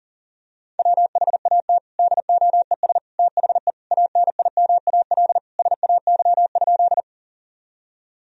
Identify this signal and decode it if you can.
Morse code, heard as a keyed tone, about 40 wpm, 700 Hz. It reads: WHAT DOES THE ANIMAL SAY?
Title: Sentences from Top 1000 Words - Review Part 1 - 40wpm